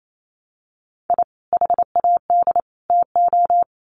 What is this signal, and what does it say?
Morse code, heard as a keyed tone, about 28 wpm, 700 Hz. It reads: I HAD TO
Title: Sets of 3 Words 28wpm